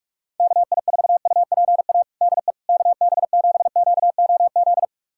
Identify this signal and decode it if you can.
Morse code, heard as a keyed tone, about 42 wpm, 700 Hz. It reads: KI4UPA DE KB7QOZ
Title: USA Callsign DE Callsign 45wpm